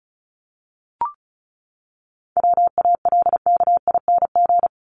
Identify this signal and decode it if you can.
Morse code, heard as a keyed tone, about 34 wpm, 700 Hz. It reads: WALKING
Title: ING Suffix - Top 100 Words with Base Words 35wpm